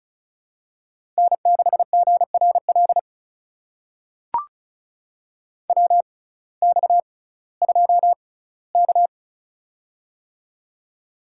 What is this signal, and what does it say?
Morse code, heard as a keyed tone, about 36 wpm, spaced as for 19 wpm, 700 Hz. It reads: N6GRL WX2K
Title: USA Call Signs - 6x Character Spacing - 35wpm